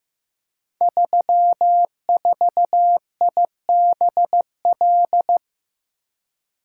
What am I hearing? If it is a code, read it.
Morse code, heard as a keyed tone, about 15 wpm, 700 Hz. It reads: 34IBL
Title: Code Groups - 5 Letter:Numbers - 15wpm